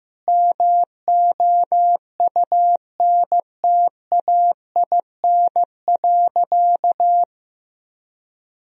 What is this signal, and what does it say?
Morse code, heard as a keyed tone, about 16 wpm, 700 Hz. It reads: MOUNTAIN.
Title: Sentences from Top 1000 Words - Review Part 1 - 15wpm